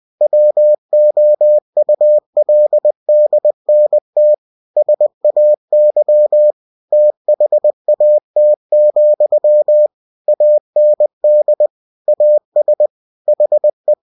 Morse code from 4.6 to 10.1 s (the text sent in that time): SAY THAT,